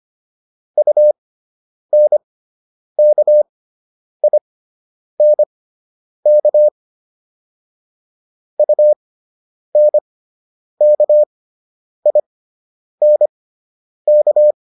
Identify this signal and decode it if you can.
Morse code, heard as a keyed tone, about 25 wpm, spaced as for 9 wpm, 600 Hz. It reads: UNKINK UNKINK